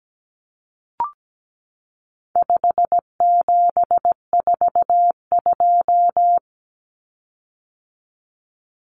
Morse code, heard as a keyed tone, about 17 wpm, 700 Hz. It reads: 5742
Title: Common Cut Numbers - 4-Digit 17wpm